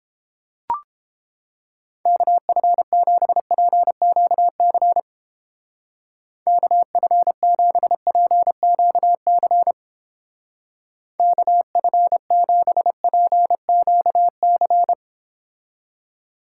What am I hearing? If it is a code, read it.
Morse code, heard as a keyed tone, about 29 wpm, 700 Hz. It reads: KF7PQC KF7PQC KF7PQC